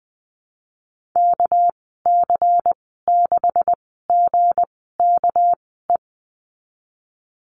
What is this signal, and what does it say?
Morse code, heard as a keyed tone, about 20 wpm, 700 Hz. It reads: K C 6 G K E
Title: 200 Lesson 32g - Letter k - Call Signs 20wpm